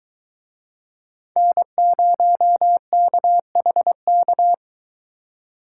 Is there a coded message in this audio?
Yes: Morse code from 1.4 to 4.5 s, heard as a keyed tone, about 22 wpm, 700 Hz. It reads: N0KHK